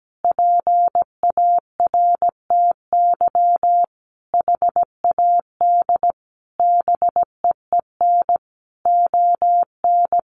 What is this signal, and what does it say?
Morse code, heard as a keyed tone, about 17 wpm, 700 Hz. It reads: PARTY HAD BEEN ON
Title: Sentences from Top 1400 Words 17wpm